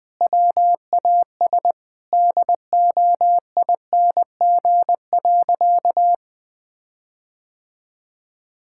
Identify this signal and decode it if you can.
Morse code, heard as a keyed tone, about 20 wpm, 700 Hz. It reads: WAS DOING.
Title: Sentences with 8-Word Phrases - 20wpm